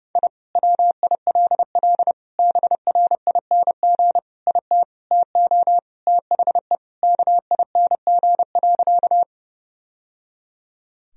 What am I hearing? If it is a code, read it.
Morse code, heard as a keyed tone, about 32 wpm, 700 Hz. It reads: I WILL BRING IT TO THE KING.
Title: Sentences from Top 1000 Words - Review Part 3 - 30wpm